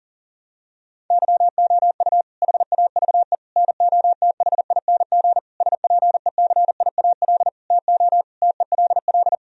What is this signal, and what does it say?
Morse code, heard as a keyed tone, about 40 wpm, 700 Hz. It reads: YOU HAVE NOTHING SPECIAL TO TELL
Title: Sentences from Top 1000 Words - Review Part 3 - 40wpm